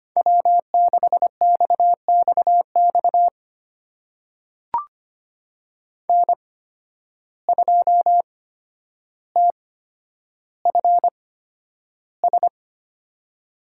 Morse code, heard as a keyed tone, about 25 wpm, spaced as for 11 wpm, 700 Hz. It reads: W6XXX N2TFS
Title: USA Call Signs - 5 Characters - 8x Character Spacing - 25wpm